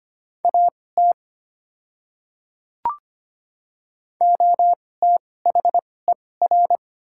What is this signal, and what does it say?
Morse code, heard as a keyed tone, about 25 wpm, spaced as for 18 wpm, 700 Hz. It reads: AT OTHER
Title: 2x Character Spacing - Top 100 Words - 25wpm